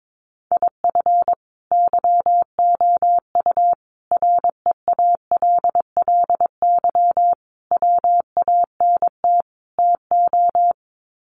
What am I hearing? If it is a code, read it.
Morse code, heard as a keyed tone, about 22 wpm, 700 Hz. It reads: IF YOU REALLY WANT TO
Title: Sets of 5 Words Encore2 22wpm